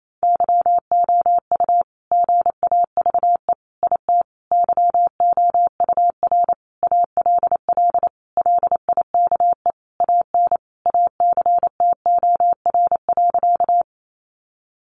Morse code, heard as a keyed tone, about 28 wpm, 700 Hz. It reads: YOU GAVE IT YOUR ALL LIKE AN ACTOR.
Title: Sentences from Top 1500 Words 28wpm